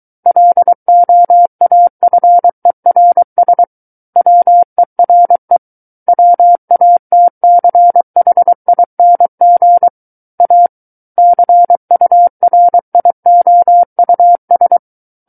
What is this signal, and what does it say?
Morse code, heard as a keyed tone, about 23 wpm, 700 Hz. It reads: LOAFERS WERE WATCHING A CURIOUS
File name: Book One / 14-Chapter 14 - 23 WPM